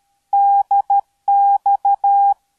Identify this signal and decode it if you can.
Morse code, heard as a keyed tone, about 13 wpm, 800 Hz. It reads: DX